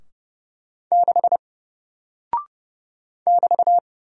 Morse code, heard as a keyed tone, about 30 wpm, 700 Hz. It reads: 6 <BT>
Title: Single Character (Letter:Number:Punctuation:Prosign) Alt 30wpm